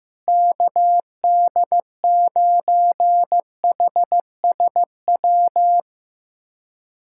Morse code, heard as a keyed tone, about 15 wpm, 700 Hz. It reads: KD9HSW
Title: USA Callsign DE Callsign - 6 Characters - 15wpm